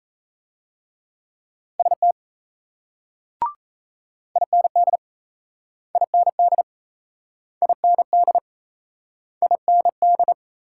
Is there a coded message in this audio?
Yes: Morse code from 1.8 to 10.3 s, heard as a keyed tone, about 36 wpm, 700 Hz. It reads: ST IND IND IND IND